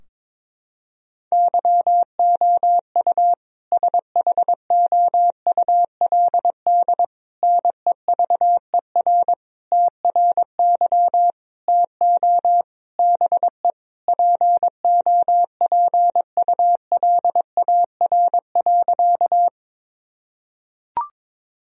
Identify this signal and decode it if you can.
Morse code, heard as a keyed tone, about 22 wpm, 700 Hz. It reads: YOU SHOULD NEVER TRY TO BE POPULAR.
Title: Sentences from Top 1000 Words - Review Part 3 - 22wpm